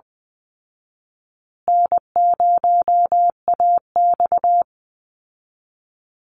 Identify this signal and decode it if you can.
Morse code, heard as a keyed tone, about 20 wpm, 700 Hz. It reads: N0AX
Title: Easy Call Signs Repeated 2x - 20wpm